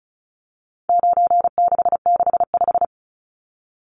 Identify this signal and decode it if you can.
Morse code, heard as a keyed tone, about 34 wpm, 700 Hz. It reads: 9665
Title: Numbers - 4-Digit 35wpm